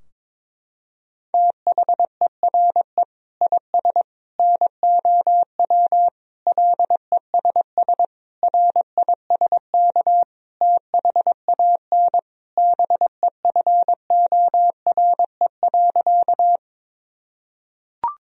Morse code, heard as a keyed tone, about 22 wpm, 700 Hz. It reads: THERE IS NOW LESS RISK THAN BEFORE.
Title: Sentences from Top 1000 Words - Review Part 4 - 22wpm